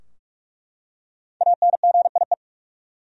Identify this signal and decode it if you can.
Morse code, heard as a keyed tone, about 48 wpm, 700 Hz. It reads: ANGIE
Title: USA Common Names - Max 5 Letters - 45wpm